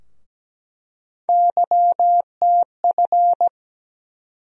Morse code, heard as a keyed tone, about 17 wpm, 700 Hz. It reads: YTF
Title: USA Call Sign Postfixes - 3 Letters 17wpm